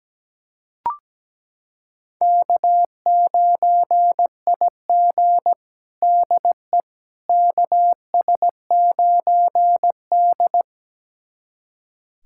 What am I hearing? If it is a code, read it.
Morse code, heard as a keyed tone, about 17 wpm, 700 Hz. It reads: K9IG DE KS9D
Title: USA Callsign DE Callsign - 4 Characters - 17wpm